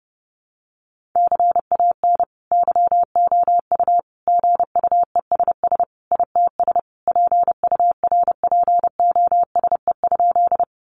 Morse code, heard as a keyed tone, about 30 wpm, 700 Hz. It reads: CAN YOU GUESS ITS PURPOSE?
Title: Sentences from Top 1500 Words 30wpm